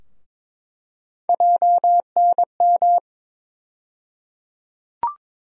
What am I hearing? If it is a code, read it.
Morse code, heard as a keyed tone, about 22 wpm, 700 Hz. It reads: JNM
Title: USA Call Sign Postfixes - 3 Letters 22wpm